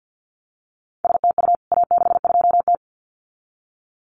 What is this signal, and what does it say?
Morse code, heard as a keyed tone, about 50 wpm, 700 Hz. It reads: STU A62T
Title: CWT Exchange 50wpm